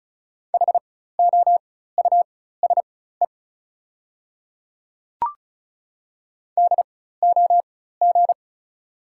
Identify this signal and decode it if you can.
Morse code, heard as a keyed tone, about 34 wpm, spaced as for 16 wpm, 690 Hz. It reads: HOUSE DOG
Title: 4x Character Spacing - Top 500 Words - 35wpm